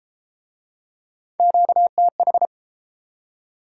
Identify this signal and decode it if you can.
Morse code, heard as a keyed tone, about 29 wpm, 700 Hz. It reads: QTH